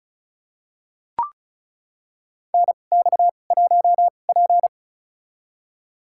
Morse code, heard as a keyed tone, about 36 wpm, spaced as for 25 wpm, 700 Hz. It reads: NX1P
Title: USA Call Signs - 4 Characters - 2x Character Spacing - 35wpm